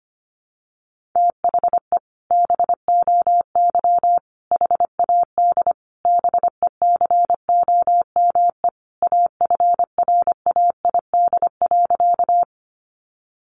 Morse code, heard as a keyed tone, about 25 wpm, 700 Hz. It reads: THE BOY HAD BECOME AFRAID.
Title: Sentences from Top 1500 Words 25wpm